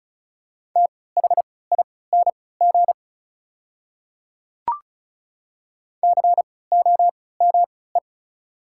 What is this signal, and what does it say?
Morse code, heard as a keyed tone, about 36 wpm, spaced as for 20 wpm, 690 Hz. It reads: THING COME
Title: 3x Character Spacing - Top 100 Words - 35wpm